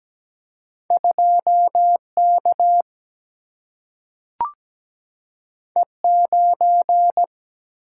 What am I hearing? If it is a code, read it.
Morse code, heard as a keyed tone, about 17 wpm, 700 Hz. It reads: 2K E9
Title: Single-Letter Single-Number 17wpm